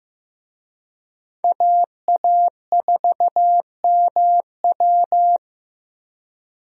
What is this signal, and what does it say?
Morse code, heard as a keyed tone, about 16 wpm, 700 Hz. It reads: AA4MW
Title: USA Call Signs - 5 Characters - Repeated 2x - 15wpm